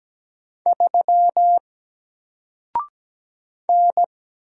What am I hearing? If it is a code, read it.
Morse code, heard as a keyed tone, about 17 wpm, 700 Hz. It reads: 3 N
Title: Single Letter-Number Alt 17wpm